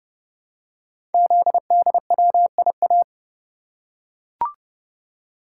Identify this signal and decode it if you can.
Morse code, heard as a keyed tone, about 30 wpm, 700 Hz. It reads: ZDWIA
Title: Code Groups - 5 Letters - 30wpm